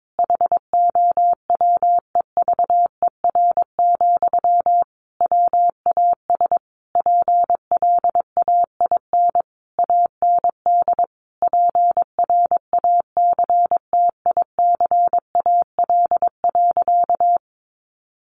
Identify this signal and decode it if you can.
Morse code, heard as a keyed tone, about 22 wpm, 700 Hz. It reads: HOWEVER, WAS PLAIN AND PRACTICAL.
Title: Sentences from Top 1400 Words 22wpm